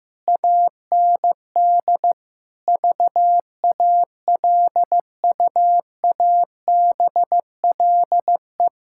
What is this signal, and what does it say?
Morse code, heard as a keyed tone, about 16 wpm, 700 Hz. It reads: AND VALUABLE
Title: Pangram Challenge v2 - 15wpm